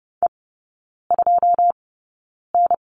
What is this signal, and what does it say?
Morse code, heard as a keyed tone, about 30 wpm, spaced as for 9 wpm, 700 Hz. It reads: E2N
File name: 081 Lesson 14e - Number 2 - Three Characters 30wpm